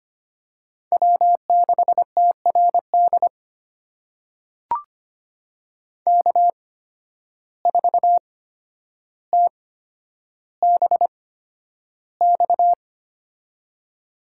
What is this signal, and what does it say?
Morse code, heard as a keyed tone, about 25 wpm, spaced as for 11 wpm, 700 Hz. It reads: W6TRD K4TBX
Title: USA Call Signs - 5 Characters - 8x Character Spacing - 25wpm